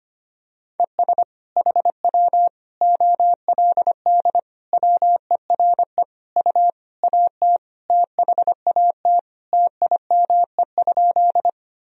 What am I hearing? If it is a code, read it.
Morse code, heard as a keyed tone, about 25 wpm, 700 Hz. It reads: ES HW OLD WERE U AT THAT TIME?
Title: Sentences from Top 200 Words and Abbreviations 25wpm